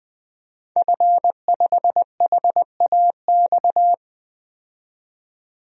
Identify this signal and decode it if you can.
Morse code, heard as a keyed tone, about 20 wpm, 700 Hz. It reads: F5HAX
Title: DXCC-100 Call Signs - 5 Characters - 20wpm